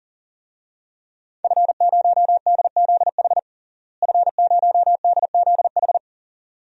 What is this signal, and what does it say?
Morse code, heard as a keyed tone, about 40 wpm, 700 Hz. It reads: F0DZH F0DZH
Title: DXCC-100 Call Signs - 5 Characters - Repeated 2x - 40wpm